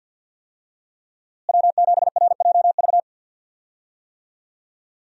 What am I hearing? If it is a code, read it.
Morse code, heard as a keyed tone, about 47 wpm, 700 Hz. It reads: W7RJV
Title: USA Call Signs - 5 Characters - Repeated 3x - 50wpm